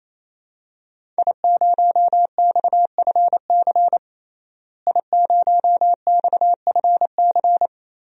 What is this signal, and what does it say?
Morse code, heard as a keyed tone, about 27 wpm, 700 Hz. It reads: I0XFC I0XFC
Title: DXCC-100 Call Signs - 5 Characters - Repeated 3x - 28wpm